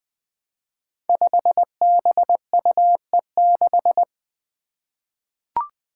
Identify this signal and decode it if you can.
Morse code, heard as a keyed tone, about 20 wpm, 700 Hz. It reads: 5BUE6
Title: Five Challenging Letter-Number Combinations 20wpm